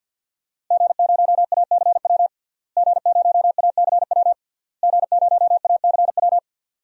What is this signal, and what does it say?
Morse code, heard as a keyed tone, about 49 wpm, 700 Hz. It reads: G0ACW G0ACW G0ACW
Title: DXCC-100 Call Signs - 5 Characters - Repeated 3x - 50wpm